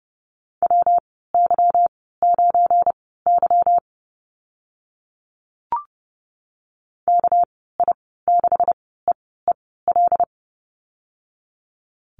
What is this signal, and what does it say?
Morse code, heard as a keyed tone, about 30 wpm, spaced as for 17 wpm, 700 Hz. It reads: WY9Y KI6EEL
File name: Lesson 37g - Letter x - Call Signs 30wpm